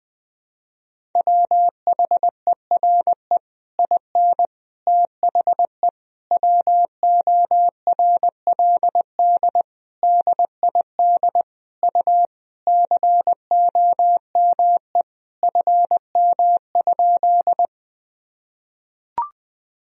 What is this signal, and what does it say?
Morse code, heard as a keyed tone, about 20 wpm, 700 Hz. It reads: WHERE IN THE WORLD DID U COME FM?
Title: Sentences from Top 200 Words and Abbreviations 20wpm